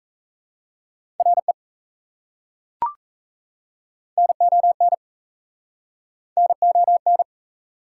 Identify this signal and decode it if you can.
Morse code, heard as a keyed tone, about 39 wpm, 700 Hz. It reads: RE NON NON